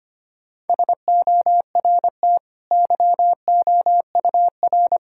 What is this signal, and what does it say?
Morse code, heard as a keyed tone, about 25 wpm, 700 Hz. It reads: SORT YOUR
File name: Easy Sentences with Hints - 25wpm